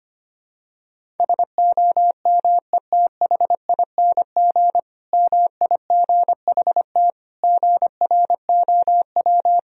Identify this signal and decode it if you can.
Morse code, heard as a keyed tone, about 25 wpm, 700 Hz. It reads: SOMETHING MIGHT GROW
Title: Sentences from Top 1000 Words - Review Part 1 - 25wpm